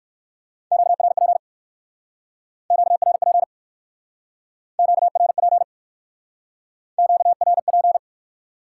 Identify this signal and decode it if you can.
Morse code, heard as a keyed tone, about 58 wpm, 700 Hz. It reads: QRP QRP QRP QRP